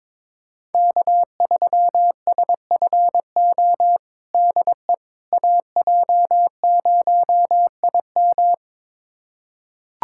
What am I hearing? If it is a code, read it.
Morse code, heard as a keyed tone, about 21 wpm, 700 Hz. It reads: K3SFO DE AJ0IM
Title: USA Callsign DE Callsign - 5 Characters - 22wpm